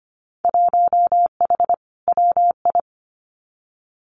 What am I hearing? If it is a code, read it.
Morse code, heard as a keyed tone, about 25 wpm, 700 Hz. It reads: 1H WI
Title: Winter Field Day Exchanges 25wpm